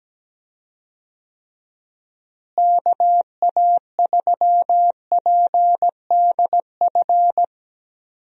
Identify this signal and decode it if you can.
Morse code, heard as a keyed tone, about 17 wpm, 700 Hz. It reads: KA3PDF
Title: USA Call Signs - 6 Characters - Repeated 3x - 17wpm